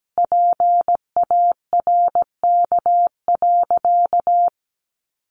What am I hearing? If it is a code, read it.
Morse code, heard as a keyed tone, about 17 wpm, 700 Hz. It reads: PARK.
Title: Sentences from Top 1300 Words 17wpm